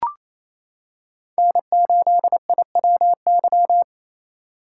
Morse code, heard as a keyed tone, about 28 wpm, 700 Hz. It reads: N8IWY
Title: K1USN SST QSO Practice - 28wpm